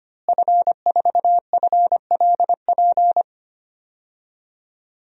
Morse code, heard as a keyed tone, about 25 wpm, 700 Hz. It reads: F4FLP
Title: DXCC-100 Call Signs - 5 Characters - Repeated 3x - 25wpm